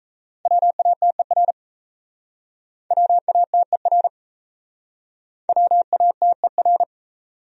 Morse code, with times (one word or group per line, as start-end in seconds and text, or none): 0.5-1.5 WATER
2.9-4.1 WATER
5.5-6.8 WATER